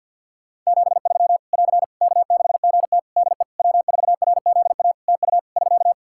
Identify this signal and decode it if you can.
Morse code, heard as a keyed tone, about 51 wpm, 700 Hz. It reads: 73 <AR> K6GT DE W4RZA TU <SK>
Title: Practice QSO - Exchange 4 - Ending - 50wpm